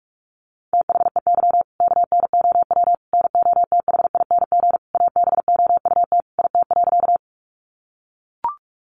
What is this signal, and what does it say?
Morse code, heard as a keyed tone, about 44 wpm, 700 Hz. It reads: THEY KNOW NOTHING ABOUT IT.
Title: Sentences from Top 500 Words 45wpm